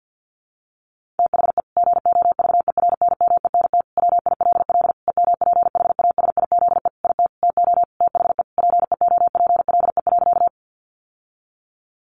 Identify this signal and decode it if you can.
Morse code, heard as a keyed tone, about 50 wpm, 700 Hz. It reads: THE GOVERNMENT WILL EMPHASIZE IT TO THE PEOPLE.
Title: Sentences from Top 1500 Words 50wpm